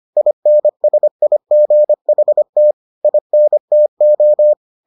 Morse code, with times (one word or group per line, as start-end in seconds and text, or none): 0.2-2.7 INSIGHT
3.0-4.5 INTO